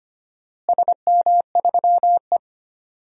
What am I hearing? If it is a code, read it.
Morse code, heard as a keyed tone, about 25 wpm, 700 Hz. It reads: SM3E
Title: DXCC-100 Call Signs - 4 Characters - Repeated 3x - 25wpm